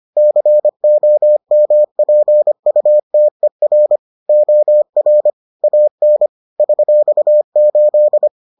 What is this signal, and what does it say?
Morse code, heard as a keyed tone, about 25 wpm, 600 Hz. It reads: COMPUTER OR AN $8